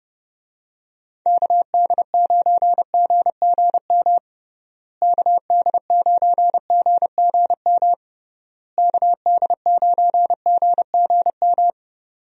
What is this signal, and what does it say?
Morse code, heard as a keyed tone, about 30 wpm, 700 Hz. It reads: KD9GGM KD9GGM KD9GGM
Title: USA Call Signs - 6 Characters - Repeated 3x - 30wpm